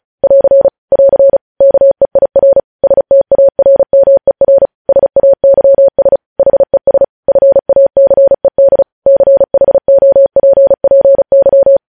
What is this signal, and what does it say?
Morse code, heard as a keyed tone, about 36 wpm, 550 Hz. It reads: <AR> <AR> KEIR STARMER SAYS HES FACED CHOPPY